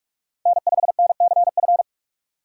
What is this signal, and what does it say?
Morse code, heard as a keyed tone, about 42 wpm, 700 Hz. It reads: N5NCF
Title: USA Call Signs - 5 Characters - 3x Character Spacing - 45wpm